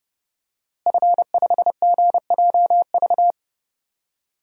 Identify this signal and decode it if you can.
Morse code, heard as a keyed tone, about 30 wpm, 700 Hz. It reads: F5GJV
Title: DXCC-100 Call Signs - 5 Characters - Repeated 3x - 30wpm